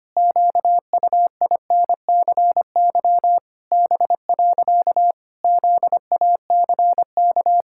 Morse code, heard as a keyed tone, about 25 wpm, 700 Hz. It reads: QUINCY B. ZACK
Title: Pangram Challenge v2 - 25wpm